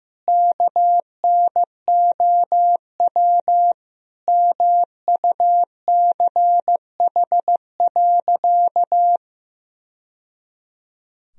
Morse code, heard as a keyed tone, about 15 wpm, 700 Hz. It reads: KNOW MUCH.
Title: Sentences from Top 1000 Words - Review Part 3 - 15wpm